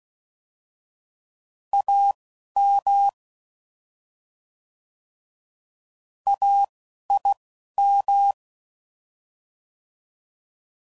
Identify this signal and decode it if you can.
Morse code, heard as a keyed tone, about 16 wpm, spaced as for 11 wpm, 800 Hz. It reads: AM AIM
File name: Prefixes / By First Letter / As